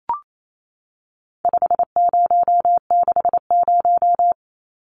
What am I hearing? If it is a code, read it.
Morse code, heard as a keyed tone, about 28 wpm, 700 Hz. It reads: 5060
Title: Numbers - 4-Digit 28wpm